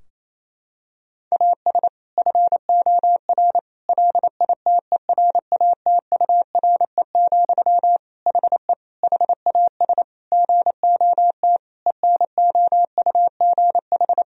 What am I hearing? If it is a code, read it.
Morse code, heard as a keyed tone, about 28 wpm, 700 Hz. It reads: AS FOR LITERATURE, HE HAS GOT ENOUGH
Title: Sentences from Top 1000 Words - Review Part 3 - 28wpm